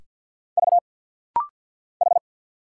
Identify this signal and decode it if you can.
Morse code, heard as a keyed tone, about 50 wpm, 700 Hz. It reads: V H
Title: Single Letters - Rapid-Fire 50wpm